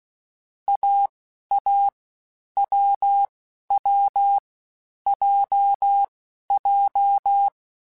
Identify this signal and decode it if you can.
Morse code, heard as a keyed tone, about 16 wpm, 800 Hz. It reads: A A W W J J